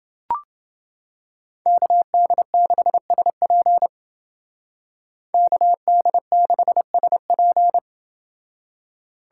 Morse code, heard as a keyed tone, about 28 wpm, 700 Hz. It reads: KD6SP KD6SP